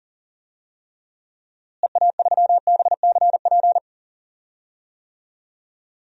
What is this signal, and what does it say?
Morse code, heard as a keyed tone, about 38 wpm, 700 Hz. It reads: EA3BCP
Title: DXCC-100 Call Signs - 6 Characters - Repeated 3x - 40wpm